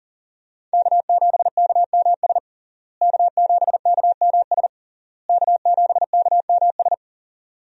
Morse code, heard as a keyed tone, about 40 wpm, 700 Hz. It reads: K7KMS K7KMS K7KMS
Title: USA Call Signs - 5 Characters - Repeated 3x - 40wpm